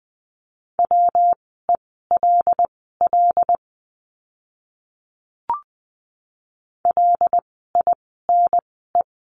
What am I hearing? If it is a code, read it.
Morse code, heard as a keyed tone, about 20 wpm, spaced as for 15 wpm, 700 Hz. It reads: WELL LINE
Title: 115 Lesson 19f - Letter w - Words 20wpm